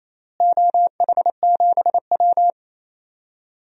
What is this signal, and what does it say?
Morse code, heard as a keyed tone, about 28 wpm, 700 Hz. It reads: OH7W
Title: DXCC-100 Call Signs - 4 Characters - Repeated 2x - 28wpm